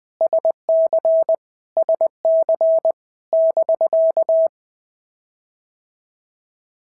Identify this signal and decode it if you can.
Morse code, heard as a keyed tone, about 20 wpm, 650 Hz. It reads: SC SC <BK>
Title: POTA QSO Practice - 20wpm